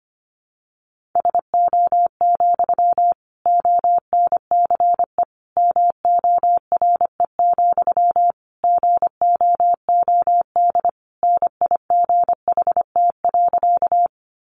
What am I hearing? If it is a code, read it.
Morse code, heard as a keyed tone, about 25 wpm, 700 Hz. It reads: SO, ONCE MORE, GOOD NIGHT.
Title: Sentences from Top 300 Words 25wpm